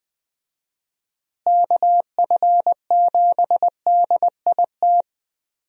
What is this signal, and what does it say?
Morse code, heard as a keyed tone, about 20 wpm, 700 Hz. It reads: KF7DIT
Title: USA Call Signs - 6 Characters - Repeated 3x - 20wpm